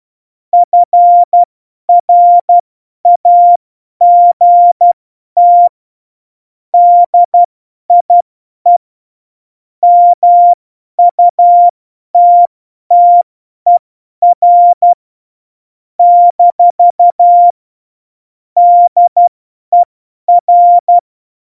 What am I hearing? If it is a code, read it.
Morse code, heard as a keyed tone, about 12 wpm, spaced as for 10 wpm, 700 Hz. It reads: FRAGT DIE MUTTER - DER
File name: Morsekurs / Witze / Witz 03 April 2018, 50 F70